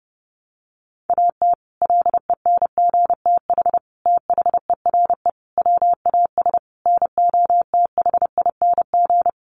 Morse code, heard as a keyed tone, about 30 wpm, 700 Hz. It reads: AT LENGTH THERE WAS NOTHING
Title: Sentences from Top 1000 Words 30wpm